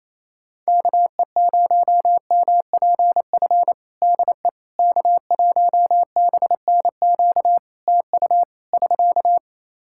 Morse code, heard as a keyed tone, about 28 wpm, 700 Hz. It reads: KE0MPF DE K1BNQ TU <SK>
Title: Practice QSO - Exchange 4 - Ending - 28wpm